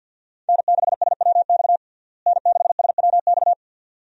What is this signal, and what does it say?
Morse code, heard as a keyed tone, about 47 wpm, 700 Hz. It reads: N6SWX N6SWX
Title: USA Call Signs - 5 Characters - Repeated 2x - 50wpm